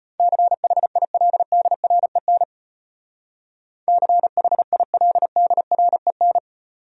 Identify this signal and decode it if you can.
Morse code, heard as a keyed tone, about 33 wpm, 700 Hz. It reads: CHILDREN CHILDREN